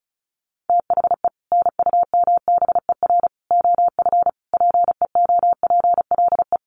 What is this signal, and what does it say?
Morse code, heard as a keyed tone, about 34 wpm, 700 Hz. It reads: THE NUMBER OF PEOPLE
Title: Sets of 5 Words Encore 35wpm